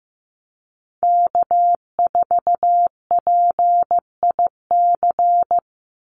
Morse code, heard as a keyed tone, about 15 wpm, 700 Hz. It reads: K4PIC
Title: NA SOTA Chaser Callsigns 15wpm